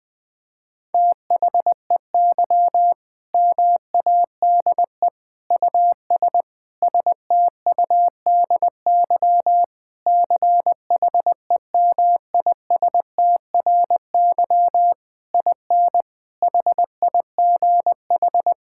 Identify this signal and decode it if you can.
Morse code, heard as a keyed tone, about 21 wpm, 700 Hz. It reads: THEY MADE US STUDY CHEMISTRY IN HIGH
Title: Sentences from Top 1000 Words - Review Part 6 - 20wpm